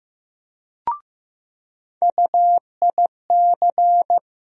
Morse code, heard as a keyed tone, about 15 wpm, 700 Hz. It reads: UIC
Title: USA Call Sign Postfixes - 3 Letters 15wpm